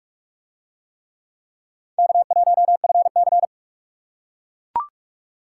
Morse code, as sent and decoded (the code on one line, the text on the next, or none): -.- .---- ..-. -.-.
K1FC